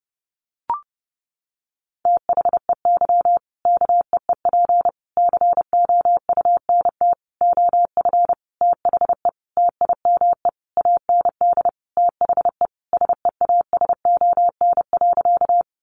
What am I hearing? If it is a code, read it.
Morse code, heard as a keyed tone, about 30 wpm, 700 Hz. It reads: THEY KEEP COUNT OF THE TIME AND THE SEASON.
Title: Sentences from Top 1500 Words 30wpm